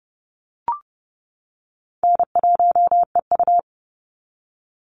Morse code, heard as a keyed tone, about 30 wpm, 700 Hz. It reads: N1EU
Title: NA SOTA Activator Callsigns 30wpm